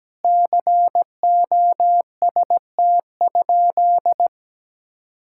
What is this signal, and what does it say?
Morse code, heard as a keyed tone, about 17 wpm, 700 Hz. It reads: COST?
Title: Sentences from Top 1000 Words - Review Part 5 - 17wpm